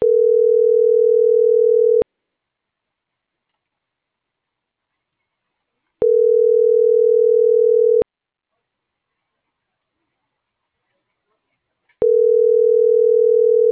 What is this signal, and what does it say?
Ringback tone (North America), 440 + 480 Hz.